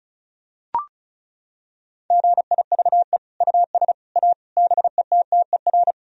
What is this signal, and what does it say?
Morse code, heard as a keyed tone, about 34 wpm, 700 Hz. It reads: GIVE US A BETTER
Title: Sentences from Top 1000 Words - Review Part 4 - 35wpm